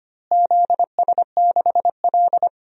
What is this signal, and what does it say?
Morse code, heard as a keyed tone, about 25 wpm, 700 Hz. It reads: ZS6L
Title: DXCC-100 Call Signs - 4 Characters - Repeated 2x - 25wpm